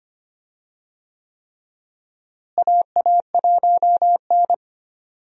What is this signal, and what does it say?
Morse code, heard as a keyed tone, about 25 wpm, 700 Hz. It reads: AA1N
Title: USA Call Signs - 4 Characters - Repeated 3x - 25wpm